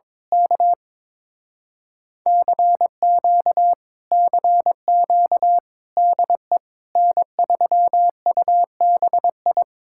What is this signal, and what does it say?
Morse code, heard as a keyed tone, about 22 wpm, 700 Hz. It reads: K CQ CQ DE N3UBI
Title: Speed-Racing Common QSO Phrases - 22wpm